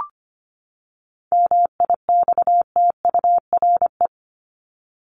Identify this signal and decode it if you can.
Morse code, heard as a keyed tone, about 25 wpm, 700 Hz. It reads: MIXTURE
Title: Top 1251-1500 Words 25wpm